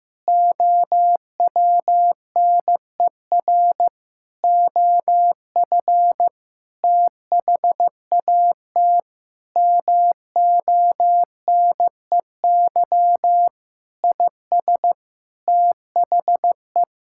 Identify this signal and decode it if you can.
Morse code, heard as a keyed tone, about 15 wpm, 700 Hz. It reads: OWNER OF THAT MONEY IS THE
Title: Sentences from Top 1600 Words 15wpm